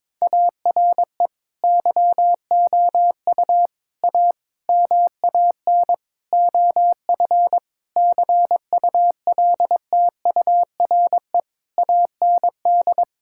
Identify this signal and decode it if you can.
Morse code, heard as a keyed tone, about 22 wpm, 700 Hz. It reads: ARE YOU A MAN OF CULTURE AND
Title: Sentences from Top 1000 Words - Review Part 6 - 22wpm